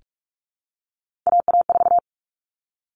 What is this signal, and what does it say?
Morse code, heard as a keyed tone, about 44 wpm, 700 Hz. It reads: AA4
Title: Common Cut Numbers - 3-Digit 45wpm